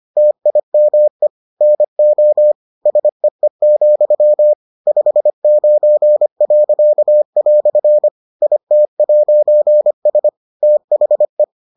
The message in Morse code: TIME NO SEE, 59." IT'S THE

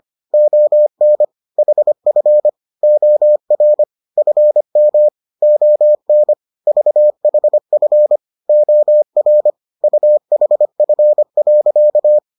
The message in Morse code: ON HF OR FM ON VHF OR UHF.

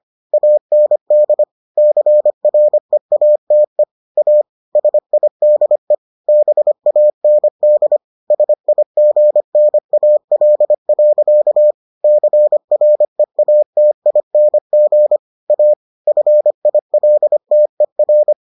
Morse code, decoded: AND CREATE A SIDE BAND SIGNAL. CREATING A FILTER

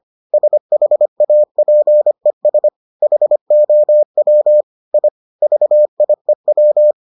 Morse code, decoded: SHAPES HOW I VIEW

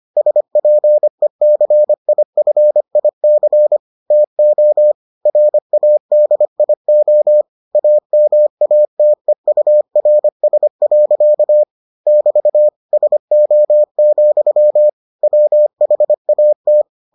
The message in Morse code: SPECIFIC TO RADIO AMATEURS. <BT> SO, WHAT

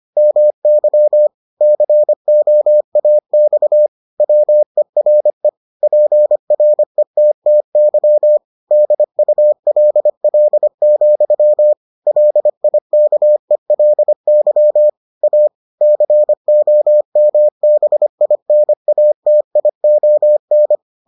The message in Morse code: MY COAX WERE PRETTY DULL, LIKELY A COMBINATION

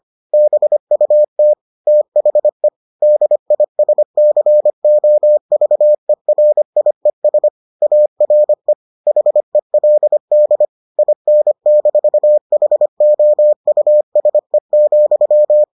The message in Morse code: BUT THE DISCOVERIES ARE HELD IN-HOUSE,

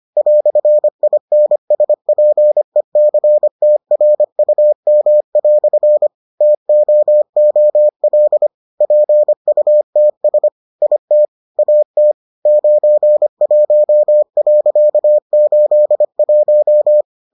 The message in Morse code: "INSPECTRUM" TOOL PUTS IT AT 91.81

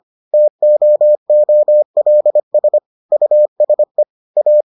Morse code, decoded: TOOLS USE A

